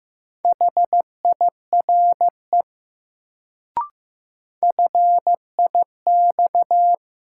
Morse code, heard as a keyed tone, about 15 wpm, 700 Hz. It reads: HIRE FIX
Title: Sentences from Top 1400 Words - New Word Drill - 15wpm